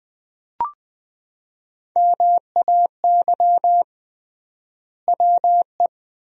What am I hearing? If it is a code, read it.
Morse code, heard as a keyed tone, about 20 wpm, 700 Hz. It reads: MAY WE
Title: 3x Word-Spacing - Sentences from Top 100 Words - 20wpm